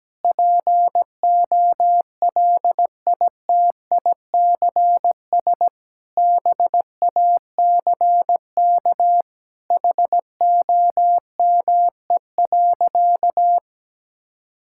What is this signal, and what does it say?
Morse code, heard as a keyed tone, about 17 wpm, 700 Hz. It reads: POLITICS BACK HOME.
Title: Sentences from Top 1000 Words - Review Part 6 - 17wpm